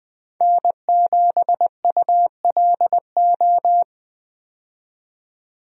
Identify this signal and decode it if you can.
Morse code, heard as a keyed tone, about 20 wpm, 700 Hz. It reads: N7ULO
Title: USA Call Signs - 5 Characters - 20wpm